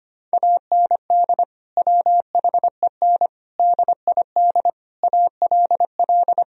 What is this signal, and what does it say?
Morse code, heard as a keyed tone, about 25 wpm, 700 Hz. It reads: AND WHEN DID ALL
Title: Sentences from Top 1000 Words - Review Part 4 - 25wpm